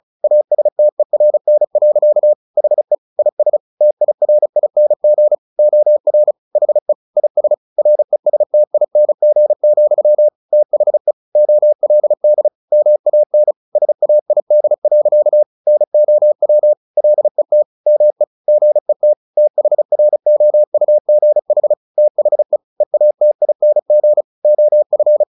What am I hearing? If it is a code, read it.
Morse code, heard as a keyed tone, about 34 wpm, 600 Hz. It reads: ASTERN. HE IS TIRING OR HE IS RESTING, THE OLD MAN SAID. NOW LET ME GET THROUGH THE EATING OF
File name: the old man and the sea 2 35wpm